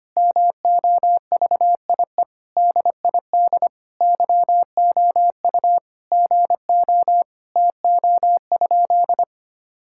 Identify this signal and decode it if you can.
Morse code, heard as a keyed tone, about 25 wpm, 700 Hz. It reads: MOVIE DID YOU GO TO?
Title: Sentences from Top 1000 Words - Review Part 5 - 25wpm